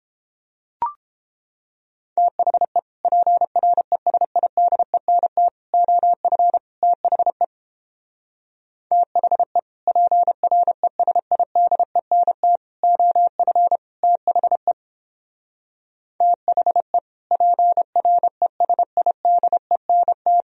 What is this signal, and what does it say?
Morse code, heard as a keyed tone, about 29 wpm, 700 Hz. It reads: THE PRESIDENT OF THE THE PRESIDENT OF THE THE PRESIDENT